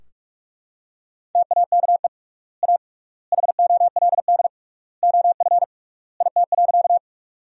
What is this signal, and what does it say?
Morse code, heard as a keyed tone, about 44 wpm, 700 Hz. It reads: TAKE A HOLD OF IT.
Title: 3x Word-Spacing - Sentences from Top 400 Words - 45wpm